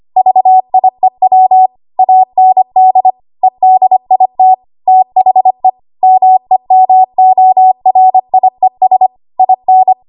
Morse code, heard as a keyed tone, about 25 wpm, 750 Hz. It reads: VIEW AND EDIT THE MEMORIES IN
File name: Archive / 25 WPM / 25WPM